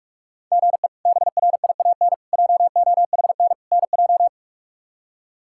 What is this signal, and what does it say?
Morse code, heard as a keyed tone, about 45 wpm, 700 Hz. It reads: GE BRIAN JOHN NJ
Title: K1USN SST QSO Practice - 45wpm